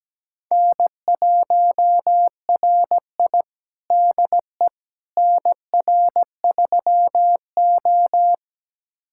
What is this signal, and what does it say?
Morse code, heard as a keyed tone, about 17 wpm, 700 Hz. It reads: N1RI DE NR3O
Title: USA Callsign DE Callsign - 4 Characters - 17wpm